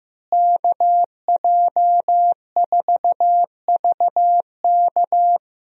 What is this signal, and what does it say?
Morse code, heard as a keyed tone, about 15 wpm, 700 Hz. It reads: KJ4VK
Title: USA Call Signs - 5 Characters - Repeated 3x - 15wpm